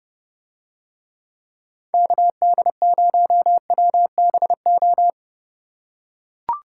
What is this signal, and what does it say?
Morse code, heard as a keyed tone, about 30 wpm, 700 Hz. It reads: KD0WBO